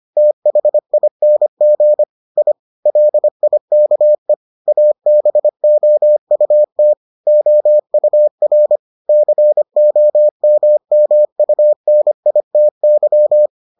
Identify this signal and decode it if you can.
Morse code, heard as a keyed tone, about 25 wpm, 600 Hz. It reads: THING I LIKE ABOUT OUR COMMUNITY